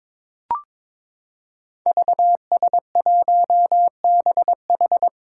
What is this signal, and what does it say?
Morse code, heard as a keyed tone, about 22 wpm, 700 Hz. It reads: VS1BH
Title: Five Challenging Letter-Number Combinations 22wpm